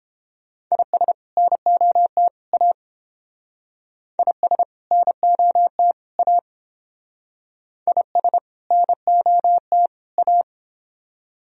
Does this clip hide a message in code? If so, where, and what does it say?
Yes: Morse code from 0.7 to 10.4 s, heard as a keyed tone, about 29 wpm, 700 Hz. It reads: IS NOT A IS NOT A IS NOT A